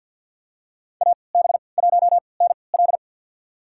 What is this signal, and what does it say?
Morse code, heard as a keyed tone, about 51 wpm, spaced as for 28 wpm, 700 Hz. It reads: AB1NL
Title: USA Call Signs - 5 Characters - 3x Character Spacing - 50wpm